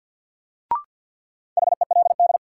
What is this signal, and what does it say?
Morse code, heard as a keyed tone, about 53 wpm, 700 Hz. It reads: HELD
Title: Sentences from Top 1600 Words - New Word Drill Alt - 50wpm